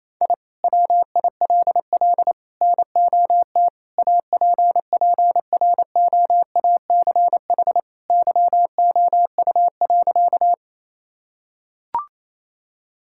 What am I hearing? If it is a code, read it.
Morse code, heard as a keyed tone, about 28 wpm, 700 Hz. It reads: I WILL NOT APPROACH YOU.
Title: Sentences from Top 1000 Words - Review Part 6 - 28wpm